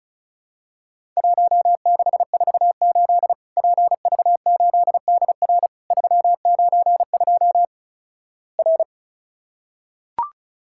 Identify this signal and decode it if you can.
Morse code, heard as a keyed tone, about 36 wpm, 700 Hz. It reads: 1648 PV8DR 392 R
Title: Worked All Europe DX Contest - QTC Exchange - 35wpm